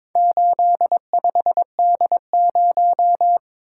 Morse code, heard as a keyed tone, about 22 wpm, 700 Hz. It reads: 85D0
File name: practice / 22wpm / Four Letter-Number Combinations - 22wpm